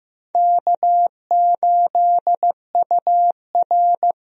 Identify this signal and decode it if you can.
Morse code, heard as a keyed tone, about 15 wpm, 700 Hz. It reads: K8UR
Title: USA Call Signs - 4 Characters - 15wpm